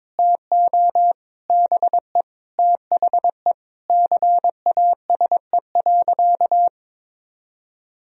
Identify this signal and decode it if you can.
Morse code, heard as a keyed tone, about 22 wpm, 700 Hz. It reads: TO BE THE CASE.
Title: Sentences from Top 1000 Words - Review Part 2 - 22wpm